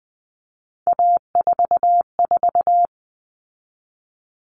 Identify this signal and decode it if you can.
Morse code, heard as a keyed tone, about 20 wpm, 700 Hz. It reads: A44
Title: All Cut Numbers - 3-Digit 20wpm